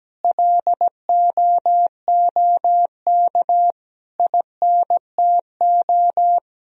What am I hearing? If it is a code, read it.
Morse code, heard as a keyed tone, about 17 wpm, 700 Hz. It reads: LOOK INTO
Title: Sentences from Top 1000 Words - Review Part 1 - 17wpm